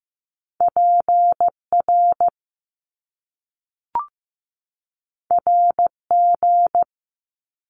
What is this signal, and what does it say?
Morse code, heard as a keyed tone, about 15 wpm, 700 Hz. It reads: PR RG
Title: Top Two Letter Combinations in Words 15wpm